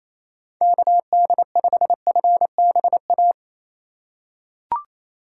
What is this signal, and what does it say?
Morse code, heard as a keyed tone, about 28 wpm, 700 Hz. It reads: KD5FBA
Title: K1USN SST QSO Practice - 28wpm